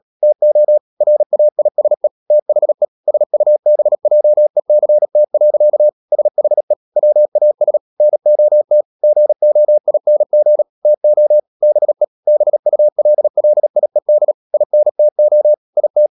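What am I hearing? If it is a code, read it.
Morse code, heard as a keyed tone, about 37 wpm, 600 Hz. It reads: TO RAISE THE SUBJECT. SHE WAS NOT GOING TO BE BULLIED INTO IT
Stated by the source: Morse code practice at 35–39 wpm sending English text of Ken Follett's novel Never, from CW med Gnister.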